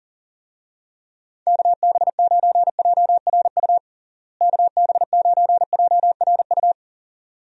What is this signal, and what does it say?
Morse code, heard as a keyed tone, about 40 wpm, 700 Hz. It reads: KB9JRU KB9JRU
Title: USA Call Signs - 6 Characters - Repeated 2x - 40wpm